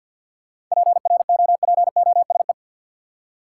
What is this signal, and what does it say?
Morse code, heard as a keyed tone, about 53 wpm, 700 Hz. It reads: PROPOSE
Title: Seven Letter Words - 50wpm